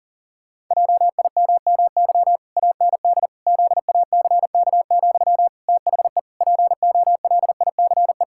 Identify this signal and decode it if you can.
Morse code, heard as a keyed tone, about 40 wpm, 700 Hz. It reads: JIMMY AND ZACK, THE POLICE
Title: Pangram Challenge v2 - 40wpm